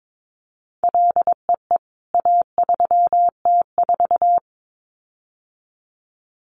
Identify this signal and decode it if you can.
Morse code, heard as a keyed tone, about 22 wpm, 700 Hz. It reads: LEE A3T4
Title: CWT Exchange 22wpm